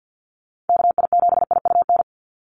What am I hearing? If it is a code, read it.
Morse code, heard as a keyed tone, about 50 wpm, 700 Hz. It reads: KI7IUN
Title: USA Call Signs 50wpm